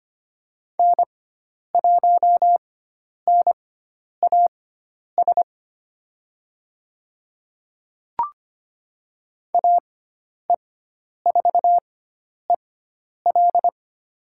Morse code, heard as a keyed tone, about 25 wpm, spaced as for 10 wpm, 700 Hz. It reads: N1NAS AE4EL
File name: Lesson 16g - Letter c - Call Signs 25wpm